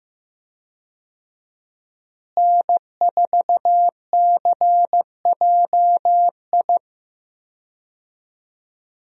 Morse code, heard as a keyed tone, about 15 wpm, 700 Hz. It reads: N4CJI
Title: USA Call Signs - 5 Characters - Repeated 2x - 15wpm